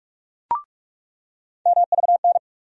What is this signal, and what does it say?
Morse code, heard as a keyed tone, about 45 wpm, 700 Hz. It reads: MVN
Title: USA Call Sign Postfixes - 3 Letters 45wpm